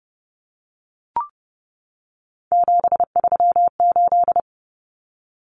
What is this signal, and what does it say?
Morse code, heard as a keyed tone, about 30 wpm, 700 Hz. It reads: 738
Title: Numbers - 3-Digit 30wpm